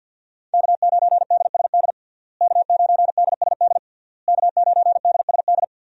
Morse code, heard as a keyed tone, about 50 wpm, 700 Hz. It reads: K9DSD K9DSD K9DSD
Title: USA Call Signs - 5 Characters - Repeated 3x - 50wpm